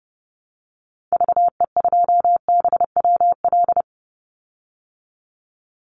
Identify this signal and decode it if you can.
Morse code, heard as a keyed tone, about 32 wpm, 700 Hz. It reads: VE2BWL
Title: Call Signs - 30wpm